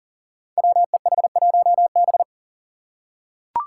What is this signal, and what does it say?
Morse code, heard as a keyed tone, about 40 wpm, 700 Hz. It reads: WEH1B
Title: Five Challenging Letter-Number Combinations 40wpm